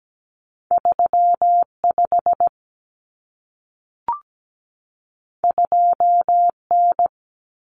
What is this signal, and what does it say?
Morse code, heard as a keyed tone, about 17 wpm, 700 Hz. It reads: 35 2N
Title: Common Cut Numbers - 2-Digit 17wpm